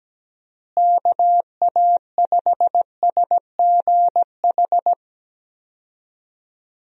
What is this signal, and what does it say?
Morse code, heard as a keyed tone, about 17 wpm, 700 Hz. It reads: KA5SGH
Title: USA Call Signs - 6 Characters - Repeated 2x - 17wpm